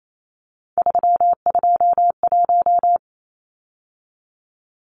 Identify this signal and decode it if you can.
Morse code, heard as a keyed tone, about 28 wpm, 700 Hz. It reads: 321
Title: Numbers - 3-Digit 28wpm